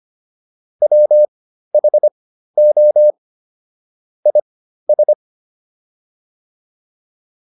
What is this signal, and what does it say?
Morse code, heard as a keyed tone, about 25 wpm, spaced as for 13 wpm, 600 Hz. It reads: WHO IS